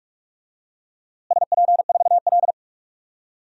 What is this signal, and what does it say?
Morse code, heard as a keyed tone, about 46 wpm, 700 Hz. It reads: SP4L
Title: DXCC-100 Call Signs - 4 Characters - Repeated 3x - 45wpm